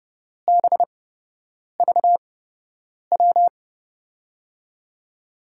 Morse code, heard as a keyed tone, about 31 wpm, spaced as for 8 wpm, 700 Hz. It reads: BVW
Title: USA Call Signs - 6 Characters - 8x Character Spacing - 30wpm